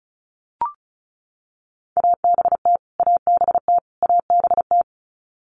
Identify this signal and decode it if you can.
Morse code, heard as a keyed tone, about 34 wpm, 700 Hz. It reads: ABT ABT ABT
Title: Abbreviations Repeated 3x - 35wpm